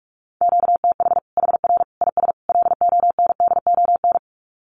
Morse code, heard as a keyed tone, about 44 wpm, 700 Hz. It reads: QTH HR IS LONDON
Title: Common QSO Phrases 45wpm